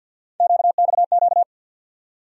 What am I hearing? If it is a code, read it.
Morse code, heard as a keyed tone, about 47 wpm, 700 Hz. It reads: QXQ
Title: USA Call Sign Postfixes - 3 Letters 50wpm